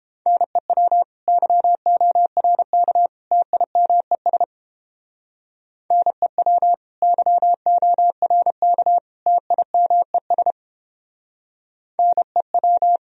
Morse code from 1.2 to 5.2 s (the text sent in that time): YORK TIMES